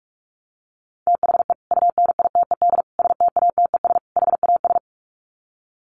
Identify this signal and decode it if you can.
Morse code, heard as a keyed tone, about 44 wpm, 700 Hz. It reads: THE UNITED STATES HAS
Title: Sets of 4 Words 45wpm